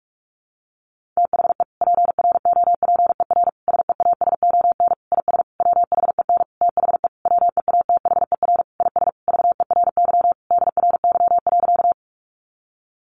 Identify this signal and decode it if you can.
Morse code, heard as a keyed tone, about 44 wpm, 700 Hz. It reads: THE PROPER SEASON IS WHEN THE WEATHER IS VERY DRY.
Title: Sentences from Top 1400 Words 45wpm